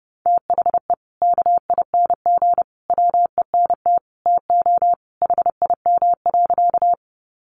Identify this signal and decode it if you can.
Morse code, heard as a keyed tone, about 28 wpm, 700 Hz. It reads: THE KING WENT TO HIM.
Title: Sentences from Top 400 Words 30wpm